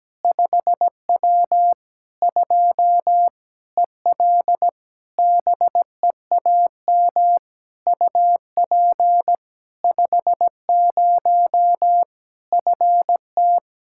Morse code, heard as a keyed tone, about 17 wpm, 700 Hz. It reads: 5W 2 EL BEAM UP 50 FT
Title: Practice QSO - Exchange 2 - Info - 17wpm